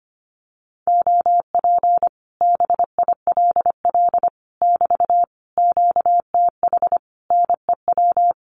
Morse code, heard as a keyed tone, about 25 wpm, 700 Hz. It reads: OP BILL <BT> QTH NEW
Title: Common QSO Phrases Encore 25wpm